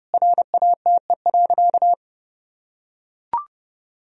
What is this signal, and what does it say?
Morse code, heard as a keyed tone, about 32 wpm, 700 Hz. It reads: RATE.
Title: Sentences from Top 1000 Words - Review Part 4 - 30wpm